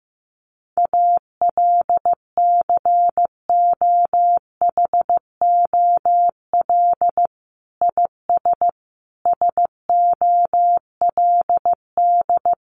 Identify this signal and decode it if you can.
Morse code, heard as a keyed tone, about 15 wpm, 700 Hz. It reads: ALCOHOL IS SOLD
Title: Sentences from Top 1400 Words 15wpm